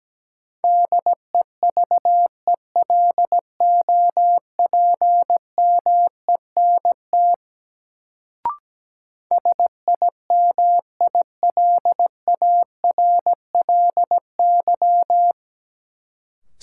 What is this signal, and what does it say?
Morse code, heard as a keyed tone, about 17 wpm, 700 Hz. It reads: DEVELOPMENT SIMILARLY
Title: Sentences from Top 1400 Words - New Word Drill Alt - 17wpm